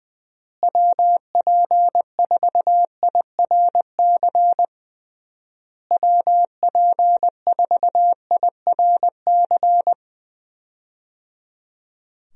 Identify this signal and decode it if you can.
Morse code, heard as a keyed tone, about 20 wpm, 700 Hz. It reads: WP4IRC WP4IRC
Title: USA Call Signs - 6 Characters - Repeated 3x - 20wpm